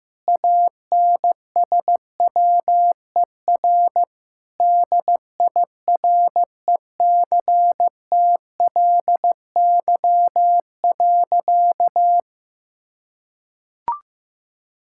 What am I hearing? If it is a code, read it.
Morse code, heard as a keyed tone, about 15 wpm, 700 Hz. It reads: ANSWER DIRECTLY.
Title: Sentences from Top 1000 Words - Review Part 5 - 15wpm